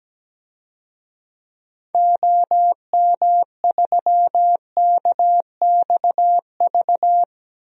Morse code, heard as a keyed tone, about 17 wpm, 700 Hz. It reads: OM3KXV
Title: DXCC-100 Call Signs - 6 Characters - 17wpm